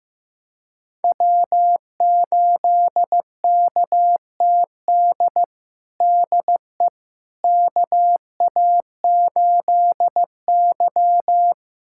Morse code, heard as a keyed tone, about 15 wpm, 700 Hz. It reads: W8KTD DE KA8Y
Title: USA Callsign DE Callsign 15wpm